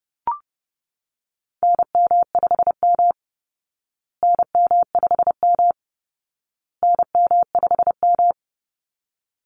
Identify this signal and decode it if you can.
Morse code, heard as a keyed tone, about 30 wpm, 700 Hz. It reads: NM5M NM5M NM5M
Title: Call Signs Repeated 2x - 30wpm